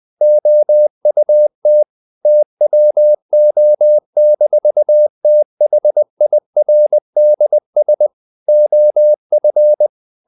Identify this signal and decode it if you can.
Morse code, heard as a keyed tone, about 20 wpm, 600 Hz. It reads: OUT TWO-THIRDS OF
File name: the old man and the sea part 2 20wpm